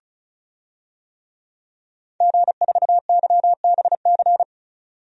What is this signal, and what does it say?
Morse code, heard as a keyed tone, about 34 wpm, 700 Hz. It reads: G4YBC
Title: DXCC-100 Call Signs - 5 Characters - 35wpm